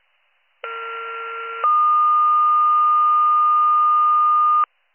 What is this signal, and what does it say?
Two-tone page, heard as a one-second tone, then a three-second tone, about 1370 then 1220 Hz. It is heard from 0.6 to 4.6 s.